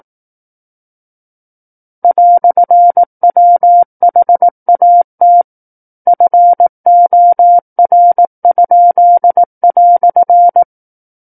Morse code, heard as a keyed tone, about 18 wpm, 700 Hz. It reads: "WHAT FOR?"
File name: 18 WPM